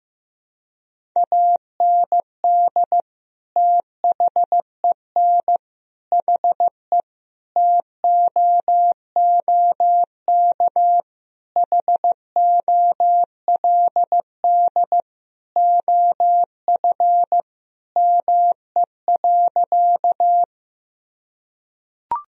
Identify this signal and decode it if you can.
Morse code, heard as a keyed tone, about 16 wpm, 700 Hz. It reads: AND THEN HE TOOK HOLD OF ME.
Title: Sentences from Top 1000 Words - Review Part 2 - 15wpm